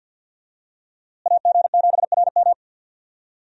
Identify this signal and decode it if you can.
Morse code, heard as a keyed tone, about 50 wpm, 700 Hz. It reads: AG7RM
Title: USA Call Signs - 5 Characters - Repeated 3x - 50wpm